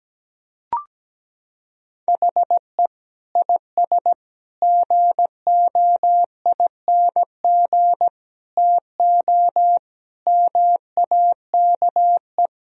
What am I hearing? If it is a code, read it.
Morse code, heard as a keyed tone, about 17 wpm, 700 Hz. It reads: HE IS GOING TO MAKE
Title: Sentences from Top 1000 Words - Review Part 4 - 17wpm